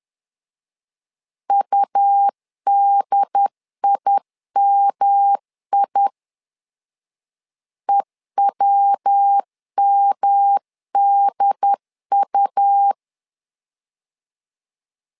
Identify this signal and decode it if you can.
Morse code, heard as a keyed tone, about 10 wpm, 800 Hz. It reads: UDIMI EWMDU
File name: LEZIONI / 3 MWRO / 3 MWR 10wpm 1952